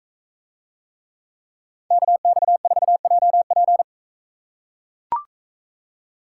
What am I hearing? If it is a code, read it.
Morse code, heard as a keyed tone, about 44 wpm, 700 Hz. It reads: KX4JP